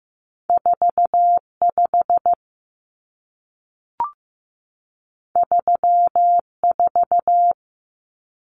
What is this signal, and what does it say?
Morse code, heard as a keyed tone, about 15 wpm, 700 Hz. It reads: 45 34
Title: Numbers - 2-Digit 15wpm